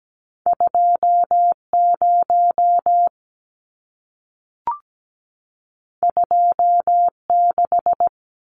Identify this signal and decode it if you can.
Morse code, heard as a keyed tone, about 17 wpm, 700 Hz. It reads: 20 26
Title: Numbers - 2-Digit 17wpm